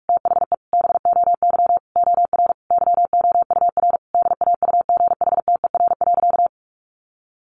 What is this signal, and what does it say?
Morse code, heard as a keyed tone, about 44 wpm, 700 Hz. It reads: THE BOY OR YOUR DAUGHTER.
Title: Sentences from Top 1500 Words 45wpm